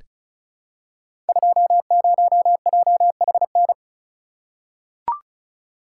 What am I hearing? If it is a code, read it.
Morse code, heard as a keyed tone, about 36 wpm, 700 Hz. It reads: 20JHN
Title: Code Groups - 5 Letter:Numbers - 35wpm